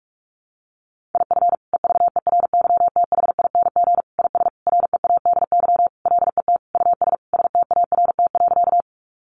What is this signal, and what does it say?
Morse code, heard as a keyed tone, about 44 wpm, 700 Hz. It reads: IF EVERYTHING IS READY LET US START.
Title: Sentences from Top 700 Words 45wpm